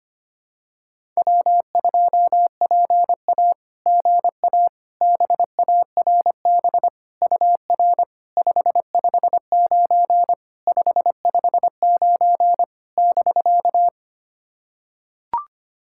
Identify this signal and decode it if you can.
Morse code, heard as a keyed tone, about 25 wpm, 700 Hz. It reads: W2PA GA BARB UR 559 559 <BK>
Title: SOTA QSO Practice 25wpm